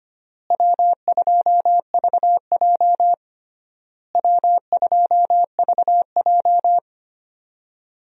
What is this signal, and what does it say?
Morse code, heard as a keyed tone, about 25 wpm, 700 Hz. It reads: W2VJ W2VJ
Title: USA Call Signs - 4 Characters - Repeated 3x - 25wpm